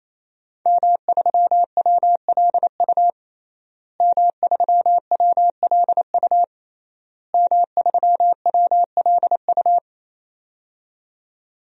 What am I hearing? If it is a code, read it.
Morse code, heard as a keyed tone, about 28 wpm, 700 Hz. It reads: M3WLU M3WLU M3WLU
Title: DXCC-100 Call Signs - 5 Characters - Repeated 3x - 28wpm